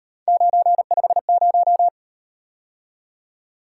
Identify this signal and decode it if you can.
Morse code, heard as a keyed tone, about 38 wpm, 700 Hz. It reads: 950